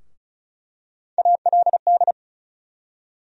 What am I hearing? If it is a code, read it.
Morse code, heard as a keyed tone, about 36 wpm, 700 Hz. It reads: ALD
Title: USA Call Sign Postfixes - 3 Letters 35wpm